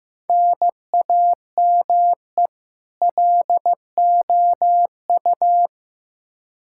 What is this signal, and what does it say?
Morse code, heard as a keyed tone, about 15 wpm, 700 Hz. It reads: NAME LOU
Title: Practice QSO - Exchange 1 - Beginning - 15wpm